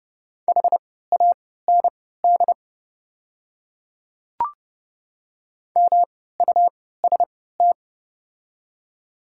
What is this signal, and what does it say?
Morse code, heard as a keyed tone, about 30 wpm, spaced as for 17 wpm, 690 Hz. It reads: HAND MUST
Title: 3x Character Spacing - Top 200 Words - 30wpm